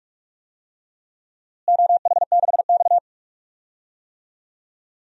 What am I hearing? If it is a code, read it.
Morse code, heard as a keyed tone, about 41 wpm, 690 Hz. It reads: OH6X
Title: DXCC-100 Call Signs - 4 Characters - Repeated 3x - 45wpm